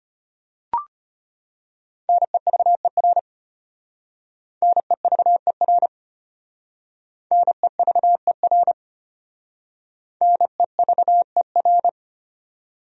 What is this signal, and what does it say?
Morse code, heard as a keyed tone, about 33 wpm, 700 Hz. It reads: NEVER NEVER NEVER NEVER